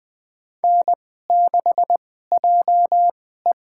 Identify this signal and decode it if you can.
Morse code, heard as a keyed tone, about 20 wpm, spaced as for 14 wpm, 700 Hz. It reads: N6JE
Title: USA Call Signs - 2x Character Spacing - 20wpm